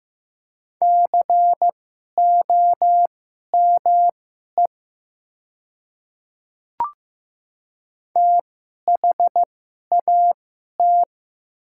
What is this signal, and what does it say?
Morse code, heard as a keyed tone, about 15 wpm, spaced as for 11 wpm, 700 Hz. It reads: COME THAT
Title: 2x Character Spacing - Top 300 Words - 15wpm